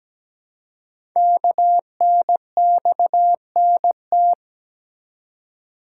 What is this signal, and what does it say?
Morse code, heard as a keyed tone, about 17 wpm, 700 Hz. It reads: KNXNT
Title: Code Groups - 5 Letters - 17wpm